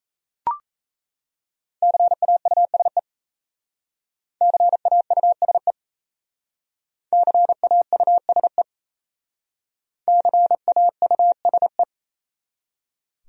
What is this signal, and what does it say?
Morse code, heard as a keyed tone, about 36 wpm, 700 Hz. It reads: CAUSE CAUSE CAUSE CAUSE